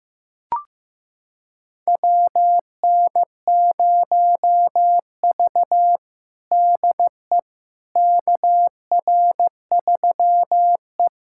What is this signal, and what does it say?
Morse code, heard as a keyed tone, about 15 wpm, 700 Hz. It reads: WN0V DE KR3E
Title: USA Callsign DE Callsign - 4 Characters - 15wpm